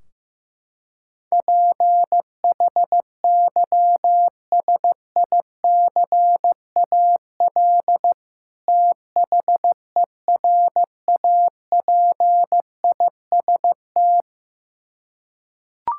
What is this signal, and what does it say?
Morse code, heard as a keyed tone, about 15 wpm, 700 Hz. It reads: PHYSICAL THERAPIST
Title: Practice QSO - Exchange 3 - More Info - 15wpm